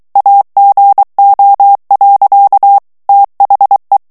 Morse code, heard as a keyed tone, about 23 wpm, 800 Hz. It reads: AGO. THE
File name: ANGLAIS texte clair 3 vitesse 22 mots minute